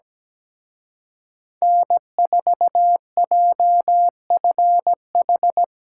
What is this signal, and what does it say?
Morse code, heard as a keyed tone, about 20 wpm, 700 Hz. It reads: N4JFH
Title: Speed Racing - USA 1x3 Call Signs 17wpm